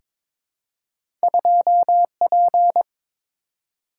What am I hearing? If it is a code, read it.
Morse code, heard as a keyed tone, about 22 wpm, 700 Hz. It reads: 2P
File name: practice / 22wpm / Two Letter-Number Combinations - 22wpm